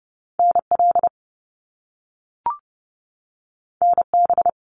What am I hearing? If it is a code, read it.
Morse code, heard as a keyed tone, about 30 wpm, 700 Hz. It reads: NL NB
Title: Canadian Provinces and Territories 30wpm